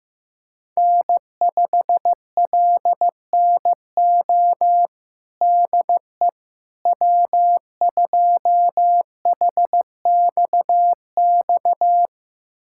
Morse code, heard as a keyed tone, about 16 wpm, 700 Hz. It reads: N5LNO DE W2HXX
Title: USA Callsign DE Callsign - 5 Characters - 15wpm